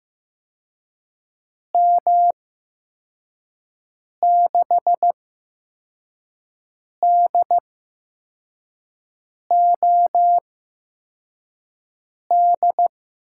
Morse code, heard as a keyed tone, about 15 wpm, spaced as for 4 wpm, 700 Hz. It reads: M6DOD
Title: USA Call Signs - 6 Characters - 8x Character Spacing - 15wpm